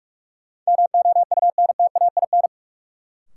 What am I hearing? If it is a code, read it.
Morse code, heard as a keyed tone, about 43 wpm, 700 Hz. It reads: MOUNTAIN
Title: Sentences from Top 1000 Words - Review Part 1 - New Word Drill - 45wpm